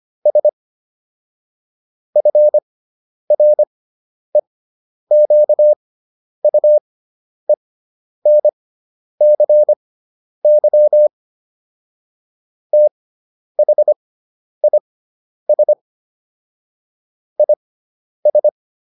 Morse code, heard as a keyed tone, about 25 wpm, spaced as for 10 wpm, 600 Hz. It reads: S FREQUENCY THIS IS